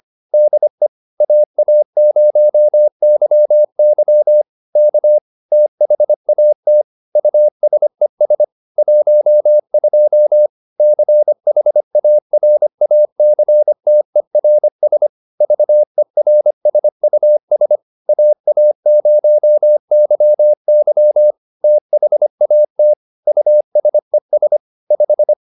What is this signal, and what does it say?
Morse code, heard as a keyed tone, about 25 wpm, 600 Hz. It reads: DE AA0YY K THAT USES 12 CHARACTERS VERSUS AA0YY THAT USES 5